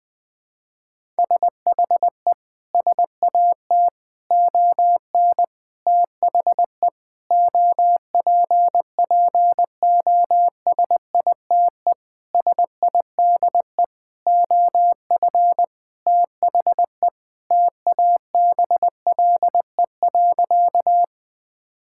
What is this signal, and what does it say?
Morse code, heard as a keyed tone, about 20 wpm, 700 Hz. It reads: SHE SAT ON THE OPPOSITE SIDE OF THE TABLE.
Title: Sentences with 6-Word Phrases - 20wpm